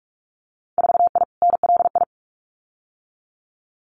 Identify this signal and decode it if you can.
Morse code, heard as a keyed tone, about 46 wpm, 700 Hz. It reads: 4I NLI
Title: Winter Field Day Exchanges Encore 45wpm